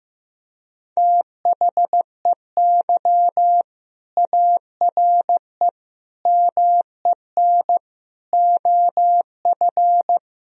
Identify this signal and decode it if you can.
Morse code, heard as a keyed tone, about 15 wpm, 700 Hz. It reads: THEY ARE MEN OF
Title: Sentences from Top 1000 Words - Review Part 2 - 15wpm